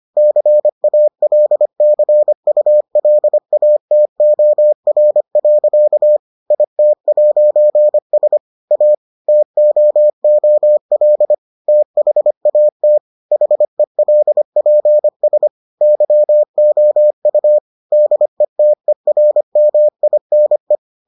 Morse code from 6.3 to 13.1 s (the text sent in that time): IT'S A TOOL THAT